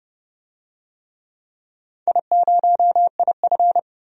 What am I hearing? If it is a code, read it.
Morse code, heard as a keyed tone, about 29 wpm, 700 Hz. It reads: I0IF
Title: DXCC-100 Call Signs - 4 Characters - 30wpm